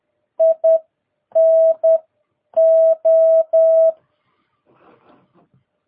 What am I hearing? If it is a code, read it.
Morse code, heard as a keyed tone, about 10 wpm, spaced as for 8 wpm, 650 Hz. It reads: INO
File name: jh8cbh / 2022cw